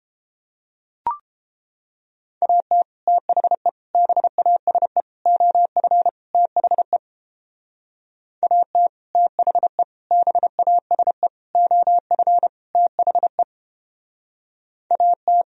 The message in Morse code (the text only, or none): AT THE BASE OF THE AT THE BASE OF THE AT